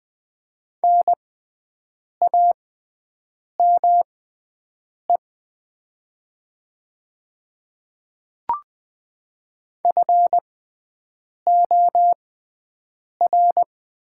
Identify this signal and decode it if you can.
Morse code, heard as a keyed tone, about 20 wpm, spaced as for 7 wpm, 700 Hz. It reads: NAME FOR
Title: 6x Character Spacing - Top 300 Words - 20wpm